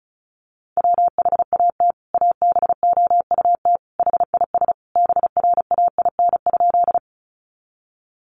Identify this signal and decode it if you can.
Morse code, heard as a keyed tone, about 34 wpm, 700 Hz. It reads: WHAT ABOUT HIS BRAIN?
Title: Sentences from Top 1100 Words 35wpm